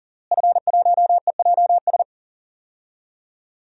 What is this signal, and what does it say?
Morse code, heard as a keyed tone, about 40 wpm, 700 Hz. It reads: F1EJS
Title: DXCC-100 Call Signs - 5 Characters - Repeated 3x - 40wpm